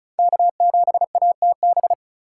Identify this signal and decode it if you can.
Morse code, heard as a keyed tone, about 36 wpm, 700 Hz. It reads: K7ATB
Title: USA Call Signs - 4x Character Spacing - 35wpm